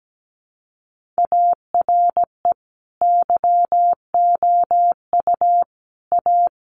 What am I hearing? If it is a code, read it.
Morse code, heard as a keyed tone, about 17 wpm, 700 Hz. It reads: ARE YOU A
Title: Sentences from Top 1500 Words 17wpm